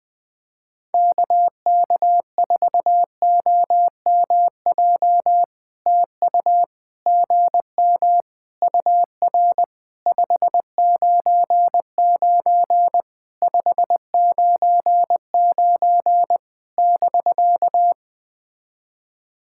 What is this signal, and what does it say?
Morse code, heard as a keyed tone, about 20 wpm, 700 Hz. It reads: KK4OMJ TU GM UR 599 599 <BK>
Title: POTA QSO Practice - 20wpm